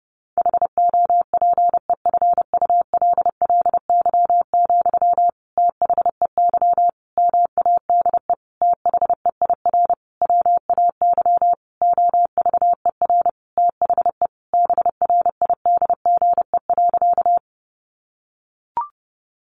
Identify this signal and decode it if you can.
Morse code, heard as a keyed tone, about 28 wpm, 700 Hz. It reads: HOPEFULLY, THEY MADE THEIR WAY OVER THE BRIDGE.
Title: Sentences from Top 1500 Words 30wpm